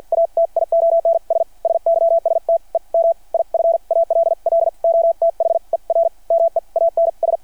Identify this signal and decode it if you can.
Morse code, heard as a keyed tone, about 53 wpm, spaced as for 42 wpm, 690 Hz. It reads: ATIONS SYSTEM IF ALL OTHER MEANS
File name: archived / CRSnet